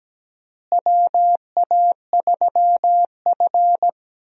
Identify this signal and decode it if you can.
Morse code, heard as a keyed tone, about 17 wpm, 700 Hz. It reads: WA3F
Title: USA Call Signs - 4 Characters - Repeated 3x - 17wpm